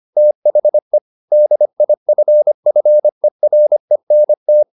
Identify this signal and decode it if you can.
Morse code, heard as a keyed tone, about 25 wpm, 600 Hz. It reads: THE DIFFERENT
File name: the old man and the sea part 1 25wpm